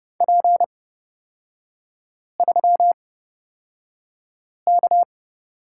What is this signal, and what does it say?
Morse code, heard as a keyed tone, about 30 wpm, spaced as for 5 wpm, 700 Hz. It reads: P3K
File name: Lesson 37e - Letter x - Three Characters 30wpm